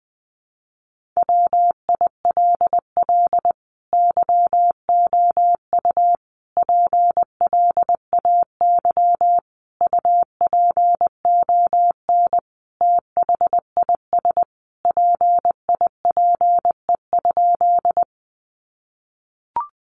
Morse code, heard as a keyed tone, about 20 wpm, 700 Hz. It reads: WILL YOU PLAY UPON THIS PIPE?
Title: Sentences from Top 1500 Words 20wpm